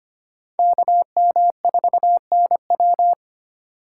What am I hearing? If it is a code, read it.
Morse code, heard as a keyed tone, about 25 wpm, 700 Hz. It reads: KM4NW
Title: USA Call Signs - 5 Characters - 3x Character Spacing - 25wpm